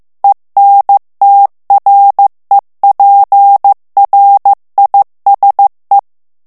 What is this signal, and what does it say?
Morse code, heard as a keyed tone, about 15 wpm, 800 Hz. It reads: ENTREPRISE
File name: morse mp3 / FRANCAIS TEXTE CLAIR 2 VITESSE 14 MOTS MINUTES